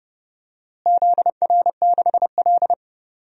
Morse code, heard as a keyed tone, about 29 wpm, 700 Hz. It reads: ZR6L
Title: DXCC-100 Call Signs - 4 Characters - Repeated 2x - 30wpm